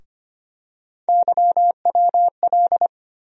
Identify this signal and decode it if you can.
Morse code, heard as a keyed tone, about 25 wpm, 700 Hz. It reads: YWL
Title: USA Call Sign Postfixes - 3 Letters 25wpm